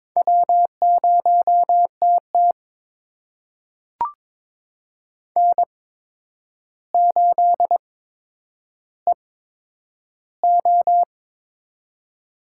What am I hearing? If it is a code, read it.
Morse code, heard as a keyed tone, about 22 wpm, spaced as for 9 wpm, 700 Hz. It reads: W0TT N8EO
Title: USA Call Signs - 4 Characters - 8x Character Spacing - 22wpm